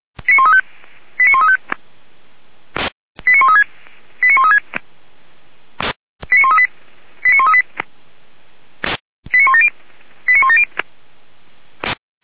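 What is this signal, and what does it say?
Five-tone selective call (ZVEI), heard as quick runs of five tones, tones of about 70 ms. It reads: 89135 89136 89136 89138 89138 89170 89170
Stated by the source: ZVEI five-tone selective call 89110 received by SDR.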